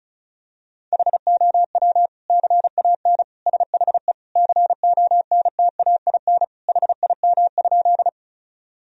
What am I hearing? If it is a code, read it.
Morse code, heard as a keyed tone, about 34 wpm, 700 Hz. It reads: HOW CAN SHE CONTAIN HIM?
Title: Sentences from Top 1000 Words - Review Part 2 - 35wpm